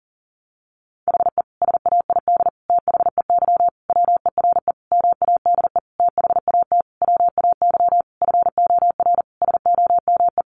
Morse code, heard as a keyed tone, about 40 wpm, 700 Hz. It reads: HE SAID THEY WERE MADE THAT WAY FOR SOME
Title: Sentences from Top 700 Words 40wpm